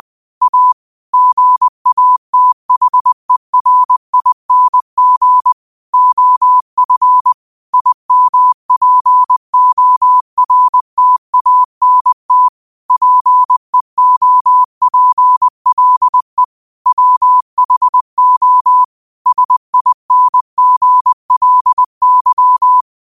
Morse code, heard as a keyed tone, about 20 wpm, 1000 Hz. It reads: A GATHERING OF IMPORTANT PEOPLE WHO SINGLY